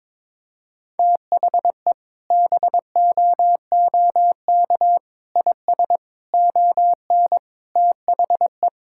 Morse code, heard as a keyed tone, about 22 wpm, 700 Hz. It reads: THE BOOK IS ON THE
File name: practice / 22wpm / Sentences with 3-Word Phrases - 22wpm